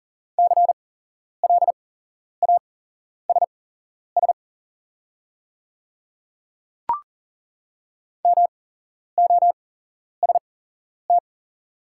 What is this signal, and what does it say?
Morse code, heard as a keyed tone, about 40 wpm, spaced as for 11 wpm, 700 Hz. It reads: CLASS MOST
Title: 8x Character Spacing - Top 500 Words - 40wpm